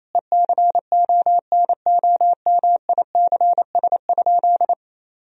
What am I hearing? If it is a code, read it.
Morse code, heard as a keyed tone, about 28 wpm, 700 Hz. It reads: ECONOMICS?
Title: Sentences from Top 1000 Words - Review Part 3 - 28wpm